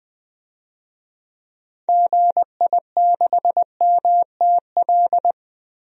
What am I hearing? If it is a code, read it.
Morse code, heard as a keyed tone, about 21 wpm, 700 Hz. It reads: GI6MTL
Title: DXCC-100 Call Signs - 6 Characters - Repeated 3x - 20wpm